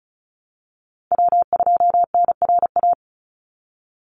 Morse code, heard as a keyed tone, about 34 wpm, 700 Hz. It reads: W2NRA
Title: Call Signs - 35wpm